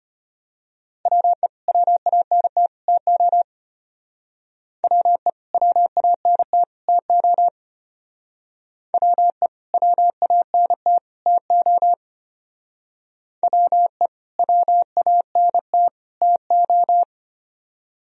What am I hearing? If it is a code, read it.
Morse code, heard as a keyed tone, about 32 wpm, 700 Hz. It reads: WE WANT TO WE WANT TO WE WANT TO WE WANT TO